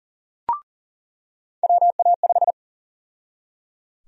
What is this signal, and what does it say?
Morse code, heard as a keyed tone, about 38 wpm, 700 Hz. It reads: WA5
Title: USA Call Sign Prefixes - 40wpm